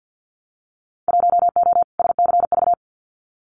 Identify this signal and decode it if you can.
Morse code, heard as a keyed tone, about 50 wpm, 700 Hz. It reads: 1O SCV
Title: Winter Field Day Exchanges 50wpm